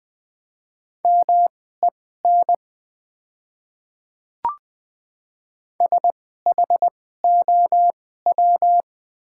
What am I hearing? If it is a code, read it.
Morse code, heard as a keyed tone, about 20 wpm, spaced as for 15 wpm, 700 Hz. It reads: MEN SHOW
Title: 2x Character Spacing - Top 300 Words - 20wpm